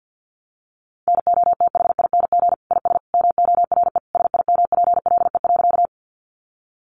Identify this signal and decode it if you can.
Morse code, heard as a keyed tone, about 50 wpm, 700 Hz. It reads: NOTHING IS MORE SIMPLE.
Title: Sentences from Top 500 Words 50wpm